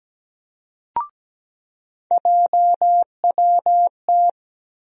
Morse code, heard as a keyed tone, about 17 wpm, 700 Hz. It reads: JWT
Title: USA Call Sign Postfixes - 3 Letters 17wpm